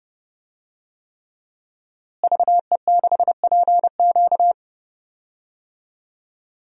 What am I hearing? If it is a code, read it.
Morse code, heard as a keyed tone, about 30 wpm, 700 Hz. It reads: VE6PQ
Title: DXCC-100 Call Signs - 5 Characters - Repeated 3x - 30wpm